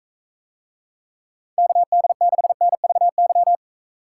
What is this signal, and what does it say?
Morse code, heard as a keyed tone, about 45 wpm, 700 Hz. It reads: KD6NVY